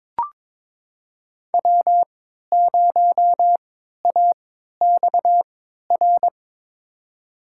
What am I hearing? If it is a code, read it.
Morse code, heard as a keyed tone, about 22 wpm, spaced as for 13 wpm, 700 Hz. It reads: W0AXR
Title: USA Call Signs - 5 Characters - 3x Character Spacing - 22wpm